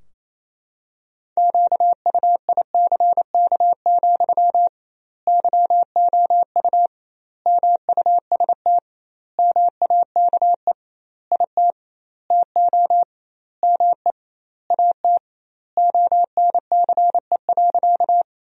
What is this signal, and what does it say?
Morse code, heard as a keyed tone, about 28 wpm, 700 Hz. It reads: QUICK, YOU MUST MAKE IT TO ME AT ONCE.
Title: 2x Word-Spacing - Sentences from Top 500 Words - 28wpm